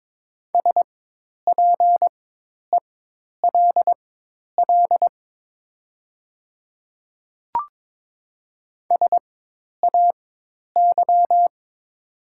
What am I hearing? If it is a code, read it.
Morse code, heard as a keyed tone, about 22 wpm, spaced as for 10 wpm, 700 Hz. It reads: SPELL SAY
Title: 4x Character Spacing - Top 200 Words - 22wpm